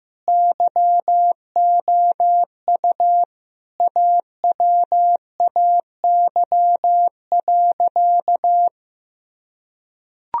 Morse code, heard as a keyed tone, about 16 wpm, 700 Hz. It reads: YOU AWAY.
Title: Sentences from Top 1000 Words - Review Part 2 - 15wpm